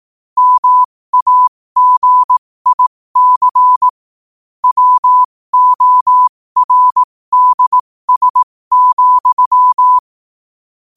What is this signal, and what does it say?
Morse code, heard as a keyed tone, about 18 wpm, spaced as for 15 wpm, 1000 Hz. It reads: MAGIC WORDS,
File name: qotdfiles / 15wpm